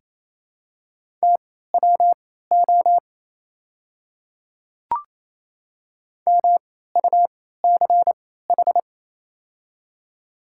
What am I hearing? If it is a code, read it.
Morse code, heard as a keyed tone, about 28 wpm, spaced as for 16 wpm, 700 Hz. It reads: TWO MUCH
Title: 3x Character Spacing - Top 200 Words - 28wpm